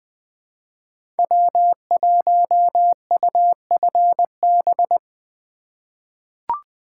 Morse code, heard as a keyed tone, about 20 wpm, 700 Hz. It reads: W1UFB
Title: USA Call Signs - 5 Characters - 3x Character Spacing - 20wpm